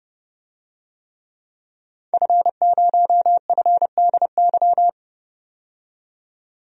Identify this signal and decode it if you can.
Morse code, heard as a keyed tone, about 32 wpm, 700 Hz. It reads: F0FDY
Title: DXCC-100 Call Signs - 5 Characters - 30wpm